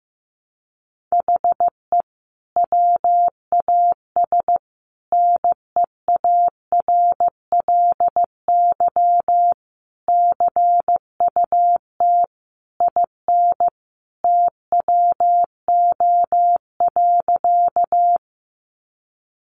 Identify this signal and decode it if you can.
Morse code, heard as a keyed tone, about 15 wpm, 700 Hz. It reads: HE WAS NEARLY CUT IN TWO.
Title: Sentences from Top 1300 Words 15wpm